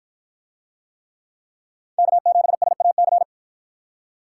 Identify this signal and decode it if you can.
Morse code, heard as a keyed tone, about 48 wpm, 700 Hz. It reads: K7SAC